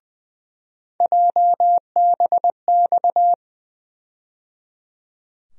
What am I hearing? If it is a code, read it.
Morse code, heard as a keyed tone, about 21 wpm, 700 Hz. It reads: JBX
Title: USA Call Sign Postfixes - 3 Letters 20wpm